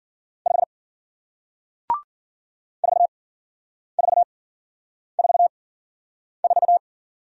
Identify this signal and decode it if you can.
Morse code, heard as a keyed tone, about 50 wpm, spaced as for 9 wpm, 700 Hz. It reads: 5 4444